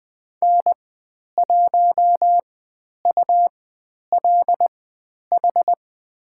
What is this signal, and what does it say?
Morse code, heard as a keyed tone, about 20 wpm, spaced as for 10 wpm, 700 Hz. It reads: N1ULH
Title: 102 Lesson 17g - Letter u - Call Signs 20wpm